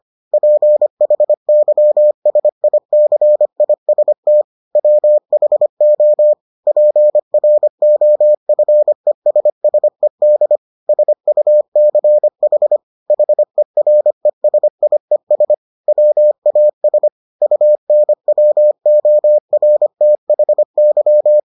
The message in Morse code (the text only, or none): PHYSICIST WHO PROFESSED SUCH HERESIES WAS UNWORTHY